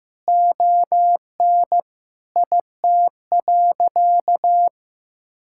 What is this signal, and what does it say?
Morse code, heard as a keyed tone, about 15 wpm, 700 Hz. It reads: ON IT.
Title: Sentences from Top 1600 Words 15wpm